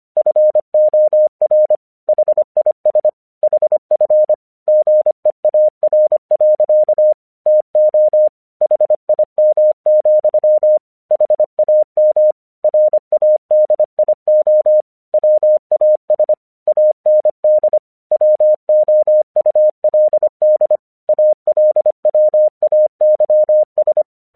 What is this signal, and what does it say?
Morse code, heard as a keyed tone, about 26 wpm, 610 Hz. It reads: FOR HIS HF GEAR. TO HIM, HAM RADIO WAS AND WOULD ALWAYS
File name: 1 English 25wpm / AQC08